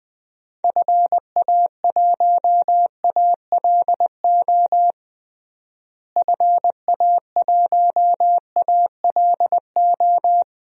Morse code, heard as a keyed tone, about 21 wpm, 700 Hz. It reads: FA1ALO FA1ALO
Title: DXCC-100 Call Signs - 6 Characters - Repeated 2x - 20wpm